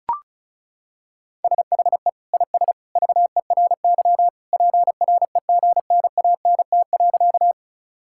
Morse code, heard as a keyed tone, about 34 wpm, 700 Hz. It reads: SHE IS VERY PREGNANT.
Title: Sentences from Top 1000 Words - Review Part 6 - 35wpm